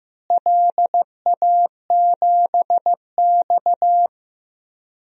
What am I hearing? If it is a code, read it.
Morse code, heard as a keyed tone, about 15 wpm, 700 Hz. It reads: LA7X
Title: DXCC-100 Call Signs - 4 Characters - Repeated 2x - 15wpm